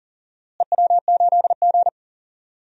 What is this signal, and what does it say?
Morse code, heard as a keyed tone, about 37 wpm, 700 Hz. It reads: EW8G
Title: DXCC-100 Call Signs - 4 Characters - Repeated 3x - 40wpm